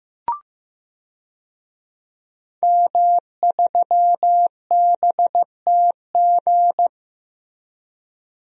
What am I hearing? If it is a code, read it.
Morse code, heard as a keyed tone, about 15 wpm, 700 Hz. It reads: M3BTG
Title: DXCC-100 Call Signs - 5 Characters - 15wpm